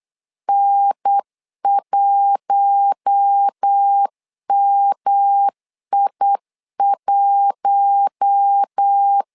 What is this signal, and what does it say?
Morse code, heard as a keyed tone, about 8 wpm, 800 Hz. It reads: N1MI1
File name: LEZIONI / 4 Q1 8wpm 3035